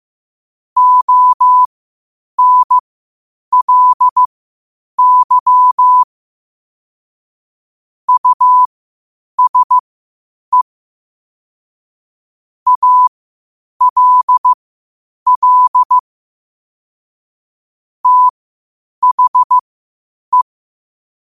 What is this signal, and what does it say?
Morse code, heard as a keyed tone, about 15 wpm, spaced as for 8 wpm, 1000 Hz. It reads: ONLY USE ALL THE
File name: qotdfiles / 8wpm